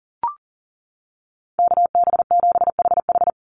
Morse code, heard as a keyed tone, about 40 wpm, 700 Hz. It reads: KB7HH
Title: NA SOTA Chaser Callsigns 40wpm